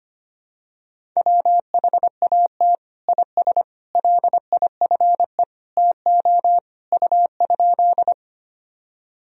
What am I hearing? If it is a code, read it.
Morse code, heard as a keyed tone, about 25 wpm, 700 Hz. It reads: WHAT IS LIFE TO U?
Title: Sentences from Top 300 Words and Abbreviations 25wpm